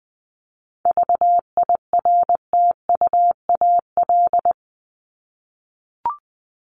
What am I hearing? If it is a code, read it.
Morse code, heard as a keyed tone, about 20 wpm, 700 Hz. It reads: VIRTUAL
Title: LY Suffix - Top 100 Words with Base Words 20wpm